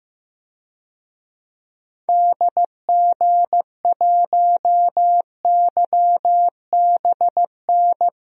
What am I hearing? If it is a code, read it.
Morse code, heard as a keyed tone, about 15 wpm, 700 Hz. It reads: DG1YBN
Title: DXCC-100 Call Signs - 6 Characters - Repeated 2x - 15wpm